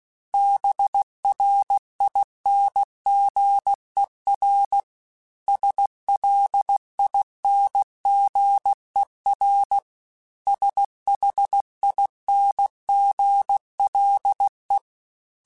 Morse code, heard as a keyed tone, about 16 wpm, 800 Hz. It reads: BRINGER SLINGER SHINGLE
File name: Special INGs-mp3s / Special INGs-mp3s / Hidden ings @15